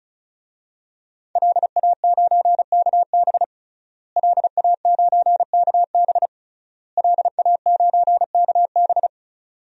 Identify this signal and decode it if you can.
Morse code, heard as a keyed tone, about 34 wpm, 700 Hz. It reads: LA9KB LA9KB LA9KB
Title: DXCC-100 Call Signs - 5 Characters - Repeated 3x - 35wpm